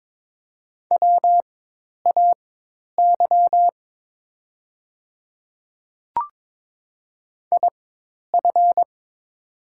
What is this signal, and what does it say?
Morse code, heard as a keyed tone, about 22 wpm, spaced as for 10 wpm, 700 Hz. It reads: WAY IF
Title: 4x Character Spacing - Top 100 Words - 22wpm